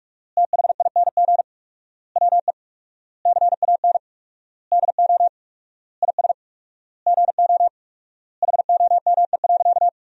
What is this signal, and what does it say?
Morse code, heard as a keyed tone, about 46 wpm, 700 Hz. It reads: THING WE CAN DO IS GO HOME.
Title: 4x Word-Spacing - Sentences from Top 200 Words - 45wpm